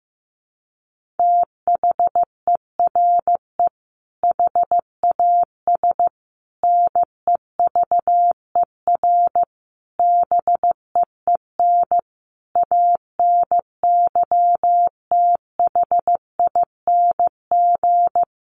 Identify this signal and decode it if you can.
Morse code, heard as a keyed tone, about 15 wpm, 700 Hz. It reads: THERE HAS NEVER BEEN ANYTHING
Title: Sentences from Top 1500 Words 15wpm